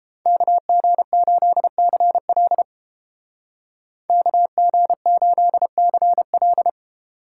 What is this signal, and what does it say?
Morse code, heard as a keyed tone, about 32 wpm, 700 Hz. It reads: KG8CL KG8CL